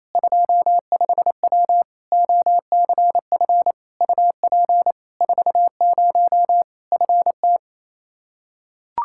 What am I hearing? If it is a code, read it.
Morse code, heard as a keyed tone, about 27 wpm, 700 Hz. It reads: 25W OCF UP 40 FT
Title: Practice QSO - Exchange 2 - Info - 28wpm